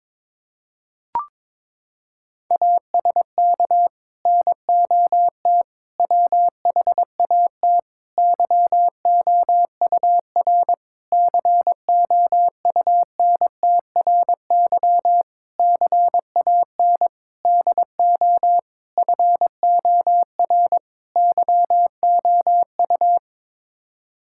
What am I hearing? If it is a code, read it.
Morse code, heard as a keyed tone, about 22 wpm, 700 Hz. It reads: ASK NOT WHAT YOUR COUNTRY CAN DO FOR YOU
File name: Sets of 9 Words - 22wpm